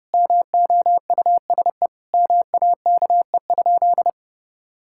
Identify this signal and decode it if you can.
Morse code, heard as a keyed tone, about 30 wpm, 700 Hz. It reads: MOUSE MAKE?
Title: Sentences from Top 1000 Words - Review Part 5 - 30wpm